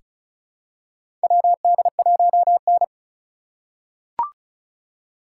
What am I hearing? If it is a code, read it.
Morse code, heard as a keyed tone, about 34 wpm, 700 Hz. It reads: WD1N
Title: USA Call Signs - 2x Character Spacing - 35wpm